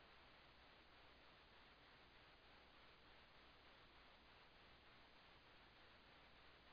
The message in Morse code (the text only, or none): none